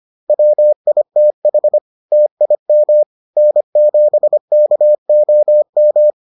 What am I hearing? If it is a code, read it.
Morse code, heard as a keyed tone, about 25 wpm, 600 Hz. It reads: WITH TIM N7KOM